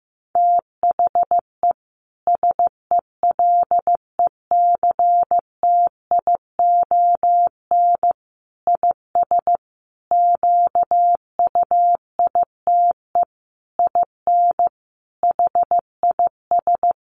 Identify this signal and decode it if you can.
Morse code, heard as a keyed tone, about 15 wpm, 700 Hz. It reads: THE SELECTION IS QUITE IN HIS
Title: Sentences from Top 1400 Words 15wpm